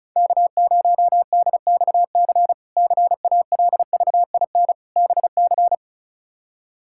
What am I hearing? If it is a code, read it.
Morse code, heard as a keyed tone, about 34 wpm, 700 Hz. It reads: K0DXC CALVIN BC
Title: K1USN SST QSO Practice - 35wpm